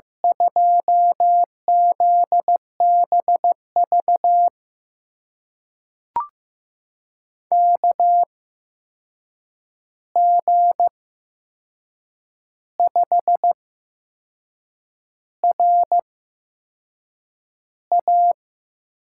Morse code, heard as a keyed tone, about 15 wpm, spaced as for 6 wpm, 700 Hz. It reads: 2ZBV KG5RA
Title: USA Call Signs - 6 Characters - 8x Character Spacing - 15wpm